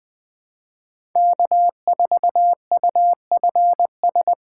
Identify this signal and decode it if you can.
Morse code, heard as a keyed tone, about 21 wpm, 700 Hz. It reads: K4UFS